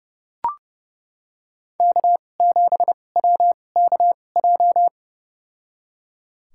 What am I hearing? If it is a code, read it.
Morse code, heard as a keyed tone, about 30 wpm, spaced as for 22 wpm, 700 Hz. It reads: K7WKJ
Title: USA Call Signs - 5 Characters - 2x Character Spacing - 30wpm